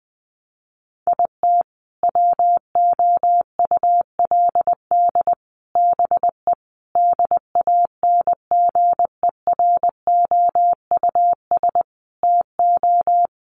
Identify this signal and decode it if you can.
Morse code, heard as a keyed tone, about 20 wpm, 700 Hz. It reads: IT WOULD BE DANGEROUS TO
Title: Sentences from Top 1200 Words 20wpm